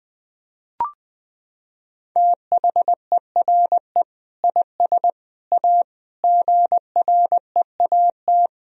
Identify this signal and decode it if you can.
Morse code, heard as a keyed tone, about 20 wpm, 700 Hz. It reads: THERE IS A GREAT
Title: Sentences from Top 1000 Words - Review Part 4 - 20wpm